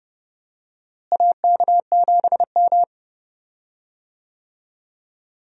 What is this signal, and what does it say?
Morse code, heard as a keyed tone, about 30 wpm, 700 Hz. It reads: AK7M
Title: USA Call Signs - 4 Characters - Repeated 2x - 30wpm